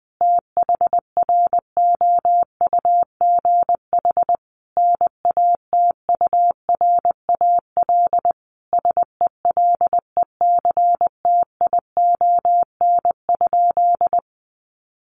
Morse code, heard as a keyed tone, about 20 wpm, 700 Hz. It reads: THROUGH NATURAL SELECTION?
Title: Sentences from Top 1400 Words 20wpm